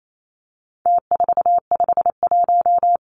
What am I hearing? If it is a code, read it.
Morse code, heard as a keyed tone, about 28 wpm, 700 Hz. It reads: T451
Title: Numbers - Contest Style - 4-Digit 28wpm